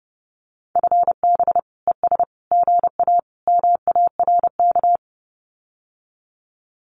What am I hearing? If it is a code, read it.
Morse code, heard as a keyed tone, about 30 wpm, 700 Hz. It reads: FB ES GA MARK
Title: Common QSO Phrases Encore 30wpm